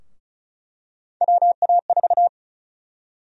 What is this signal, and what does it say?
Morse code, heard as a keyed tone, about 34 wpm, 700 Hz. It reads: WA4
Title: USA Call Sign Prefixes - 35wpm